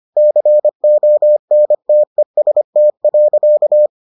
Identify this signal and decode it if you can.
Morse code, heard as a keyed tone, about 25 wpm, 600 Hz. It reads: CONTEST.